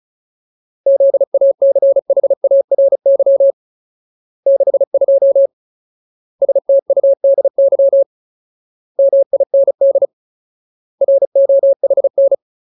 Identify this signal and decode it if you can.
Morse code, heard as a keyed tone, about 34 wpm, 550 Hz. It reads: ZACHARY 62 STUDY MIND ROHN
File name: Ultimate Word List / 35WPM Ultimate